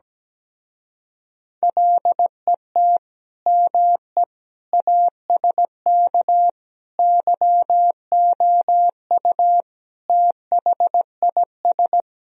Morse code, heard as a keyed tone, about 17 wpm, 700 Hz. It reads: LET ME ASK YOU THIS
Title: Speed-Racing - Sets of 5 Words 17wpm